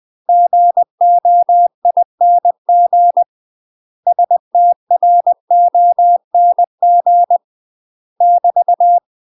Morse code, heard as a keyed tone, about 20 wpm, 700 Hz. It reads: GOING STRONG <BT>
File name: UK MASTER QSOs TRAINING / 20WPM / UK MASTER QSOs 0032